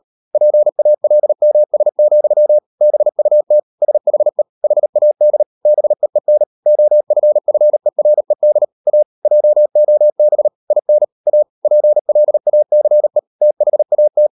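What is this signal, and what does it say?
Morse code, heard as a keyed tone, about 38 wpm, 600 Hz. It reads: PALMS, BUT SHE HAD BEEN OFFERED A JOB IN A PLACE THAT
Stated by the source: Morse code practice at 38 WPM, sending Ken Follett's novel Never in English, from CW med Gnister.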